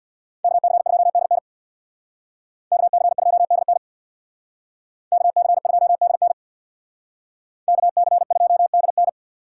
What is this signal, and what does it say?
Morse code, heard as a keyed tone, about 59 wpm, 700 Hz. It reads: KC2DN KC2DN KC2DN KC2DN
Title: Speed Racing - USA 2x2 Call Signs 50wpm